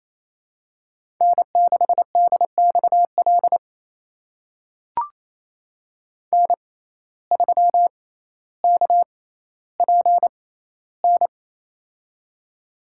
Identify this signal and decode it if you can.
Morse code, heard as a keyed tone, about 28 wpm, spaced as for 14 wpm, 700 Hz. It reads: N6DXL N3KPN
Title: USA Call Signs - 6x Character Spacing - 28wpm